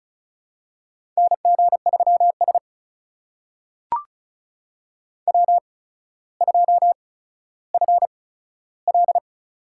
Morse code, heard as a keyed tone, about 36 wpm, spaced as for 15 wpm, 700 Hz. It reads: NG3S W2FL
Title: USA Call Signs - 4 Characters - 8x Character Spacing - 35wpm